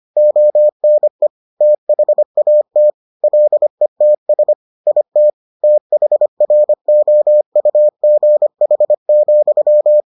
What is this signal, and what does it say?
Morse code, heard as a keyed tone, about 25 wpm, 600 Hz. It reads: ONE THAT LETS IT THROUGH,